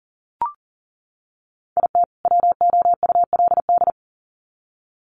Morse code, heard as a keyed tone, about 40 wpm, 700 Hz. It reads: IT WOULD
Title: Sets of 2 Words 40wpm